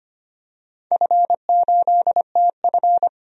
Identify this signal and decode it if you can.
Morse code, heard as a keyed tone, about 25 wpm, 700 Hz. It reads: F8TF
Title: DXCC-100 Call Signs - 4 Characters - 25wpm